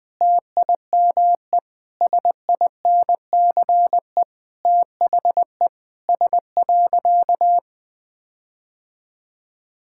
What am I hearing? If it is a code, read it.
Morse code, heard as a keyed tone, about 20 wpm, 700 Hz. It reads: TIME SINCE THE S.
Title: Sentences with 6-Word Phrases - 20wpm